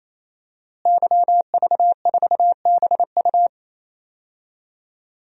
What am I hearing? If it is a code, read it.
Morse code, heard as a keyed tone, about 28 wpm, 700 Hz. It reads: YV4BU
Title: DXCC-100 Call Signs - 5 Characters - Repeated 3x - 28wpm